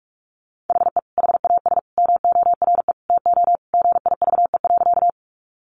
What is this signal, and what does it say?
Morse code, heard as a keyed tone, about 44 wpm, 700 Hz. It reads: HE HAS MORE TO GIVE.
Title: Sentences from Top 200 Words 45wpm